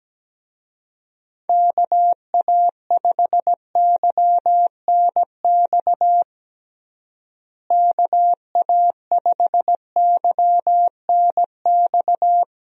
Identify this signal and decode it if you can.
Morse code, heard as a keyed tone, about 17 wpm, 700 Hz. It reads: KA5YNX KA5YNX
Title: USA Call Signs - 6 Characters - Repeated 2x - 17wpm